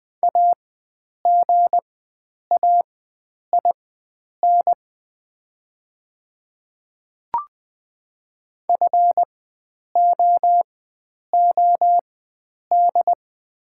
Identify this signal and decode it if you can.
Morse code, heard as a keyed tone, about 20 wpm, spaced as for 9 wpm, 700 Hz. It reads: AGAIN FOOD
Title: 4x Character Spacing - Top 300 Words - 20wpm